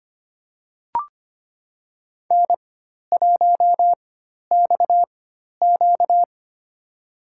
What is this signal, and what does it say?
Morse code, heard as a keyed tone, about 25 wpm, spaced as for 12 wpm, 690 Hz. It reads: N1XQ
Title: USA Call Signs - 4 Characters - 4x Character Spacing - 25wpm